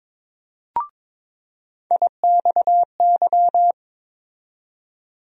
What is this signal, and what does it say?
Morse code, heard as a keyed tone, about 22 wpm, 700 Hz. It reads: IXY
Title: USA Call Sign Postfixes - 3 Letters 22wpm